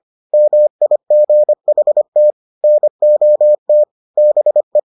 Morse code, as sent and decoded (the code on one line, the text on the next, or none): -- .. --. .... - / -. --- - / -... .
MIGHT NOT BE